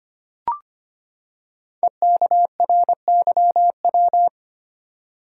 Morse code, heard as a keyed tone, about 25 wpm, 700 Hz. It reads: EKRYW
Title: Code Groups - 5 Letters - 25wpm